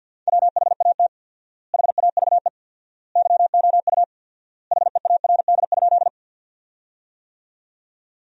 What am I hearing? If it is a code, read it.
Morse code, heard as a keyed tone, about 50 wpm, 700 Hz. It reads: WHAT HAVE YOU HEARD?
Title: 4x Word-Spacing - Sentences from Top 400 Words - 50wpm